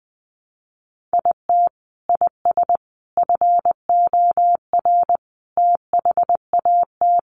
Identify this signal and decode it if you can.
Morse code, heard as a keyed tone, about 20 wpm, 700 Hz. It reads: IT IS FOR THAT
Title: Sentences from Top 700 Words 20wpm